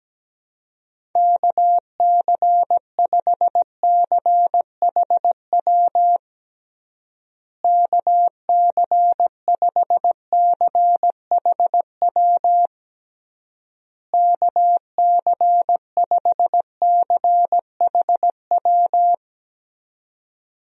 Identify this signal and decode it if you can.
Morse code, heard as a keyed tone, about 17 wpm, 700 Hz. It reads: KC5CHW KC5CHW KC5CHW
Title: USA Call Signs - 6 Characters - Repeated 3x - 17wpm